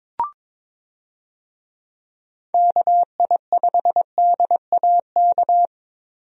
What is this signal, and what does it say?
Morse code, heard as a keyed tone, about 22 wpm, 700 Hz. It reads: KI5DAK
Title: USA Call Signs - 6 Characters - Repeated 3x - 22wpm